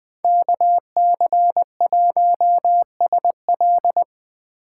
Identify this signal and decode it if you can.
Morse code, heard as a keyed tone, about 20 wpm, 700 Hz. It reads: KC1SL